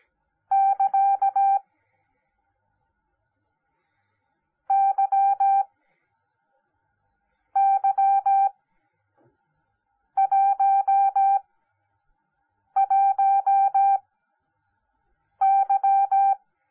Morse code, heard as a keyed tone, about 17 wpm, spaced as for 5 wpm, 790 Hz. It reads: <KA> YY11Y
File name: Onlinelehrgang / morsen / DJ4UF Morsen10